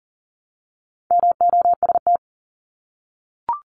Morse code, heard as a keyed tone, about 40 wpm, 700 Hz. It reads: MOST
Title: Multi-letter Phonemes 40wpm